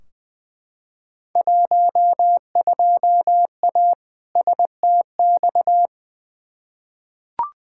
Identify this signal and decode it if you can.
Morse code, heard as a keyed tone, about 20 wpm, 700 Hz. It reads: 12A STX
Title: Field Day Exchanges Encore - 20wpm